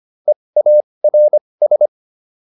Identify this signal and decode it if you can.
Morse code, heard as a keyed tone, about 25 wpm, spaced as for 20 wpm, 600 Hz. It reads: EARS